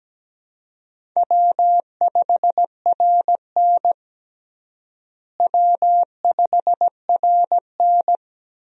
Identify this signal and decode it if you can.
Morse code, heard as a keyed tone, about 17 wpm, 700 Hz. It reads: W5RN W5RN
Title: USA Call Signs - 4 Characters - Repeated 2x - 17wpm